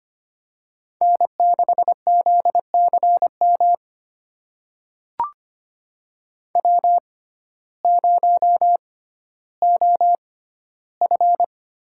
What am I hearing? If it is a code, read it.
Morse code, heard as a keyed tone, about 25 wpm, spaced as for 14 wpm, 700 Hz. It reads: N6ZCM W0OF
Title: USA Call Signs - 5 Characters - 6x Character Spacing - 25wpm